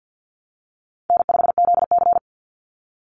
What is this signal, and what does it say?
Morse code, heard as a keyed tone, about 50 wpm, 700 Hz. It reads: N5ZC
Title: Call Signs Repeated 3x - 50wpm